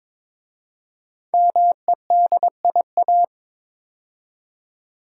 Morse code, heard as a keyed tone, about 22 wpm, 700 Hz. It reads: MEDIA
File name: 5 Letter Words - First 3 Letters Given - 22wpm